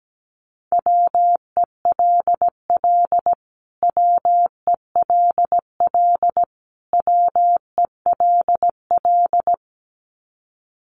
Morse code, heard as a keyed tone, about 17 wpm, 700 Hz. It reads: WELL WELL WELL
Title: Top 500 Words Repeated 17wpm